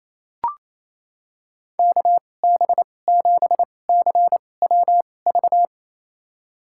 Morse code, heard as a keyed tone, about 28 wpm, spaced as for 20 wpm, 700 Hz. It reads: KB7CWV
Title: USA Call Signs - 6 Characters - 2x Character Spacing - 28wpm